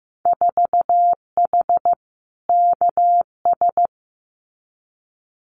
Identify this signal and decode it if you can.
Morse code, heard as a keyed tone, about 15 wpm, 700 Hz. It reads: 4H KS
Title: Winter Field Day Exchanges Encore 15wpm